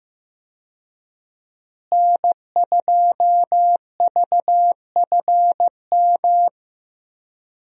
Morse code, heard as a keyed tone, about 15 wpm, 700 Hz. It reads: N2VFM
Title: USA Call Signs - 5 Characters - Repeated 3x - 15wpm